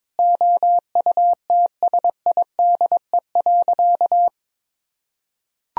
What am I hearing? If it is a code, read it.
Morse code, heard as a keyed tone, about 22 wpm, 700 Hz. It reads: OUTSIDE.
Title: Sentences from Top 1000 Words - Review Part 4 - 22wpm